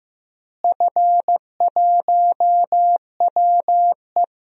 Morse code, heard as a keyed tone, about 15 wpm, 700 Hz. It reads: F1WE
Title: DXCC-100 Call Signs - 4 Characters - Repeated 2x - 15wpm